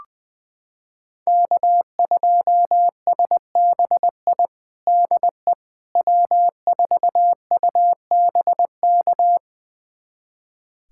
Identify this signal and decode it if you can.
Morse code, heard as a keyed tone, about 21 wpm, 700 Hz. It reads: K2SBI DE W4UBK
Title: USA Callsign DE Callsign - 5 Characters - 20wpm